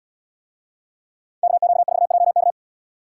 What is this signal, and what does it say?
Morse code, heard as a keyed tone, about 71 wpm, 700 Hz. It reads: KQ4JK